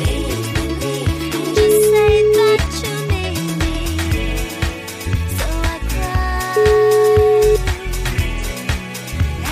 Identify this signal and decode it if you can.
Ringback tone (Europe, 425 Hz), with noise.